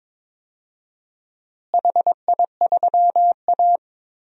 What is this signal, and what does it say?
Morse code, heard as a keyed tone, about 22 wpm, 700 Hz. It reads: HI3A
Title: DXCC-100 Call Signs - 4 Characters - Repeated 2x - 22wpm